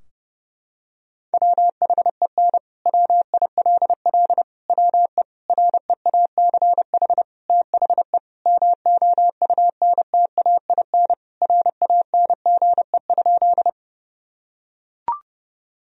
Morse code, heard as a keyed tone, about 30 wpm, 700 Hz. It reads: WHEN WILL WE REACH THE MOUNTAIN RANGE?
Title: Sentences from Top 1000 Words - Review Part 4 - 30wpm